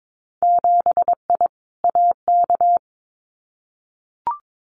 Morse code, heard as a keyed tone, about 22 wpm, 700 Hz. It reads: 7I AK
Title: Winter Field Day Exchanges Encore 22wpm